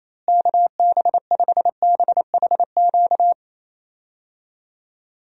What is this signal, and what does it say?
Morse code, heard as a keyed tone, about 28 wpm, 700 Hz. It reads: KB5BHQ
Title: USA Call Signs - 6 Characters - Repeated 3x - 28wpm